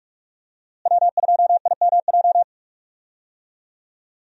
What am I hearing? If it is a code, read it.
Morse code, heard as a keyed tone, about 46 wpm, 700 Hz. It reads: W2IMJ
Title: USA Call Signs - 5 Characters - Repeated 3x - 45wpm